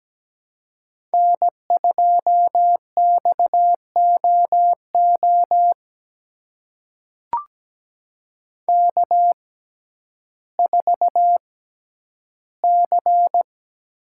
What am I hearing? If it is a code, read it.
Morse code, heard as a keyed tone, about 17 wpm, spaced as for 10 wpm, 700 Hz. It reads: N2XOO K4C
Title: USA Call Signs - 5 Characters - 6x Character Spacing - 17wpm